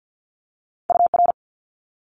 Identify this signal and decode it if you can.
Morse code, heard as a keyed tone, about 50 wpm, 700 Hz. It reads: UR
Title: Multi-letter Phonemes Quick Fire 50wpm